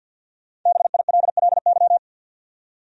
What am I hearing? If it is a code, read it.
Morse code, heard as a keyed tone, about 48 wpm, 700 Hz. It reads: BILLY
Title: USA Common Names - 50wpm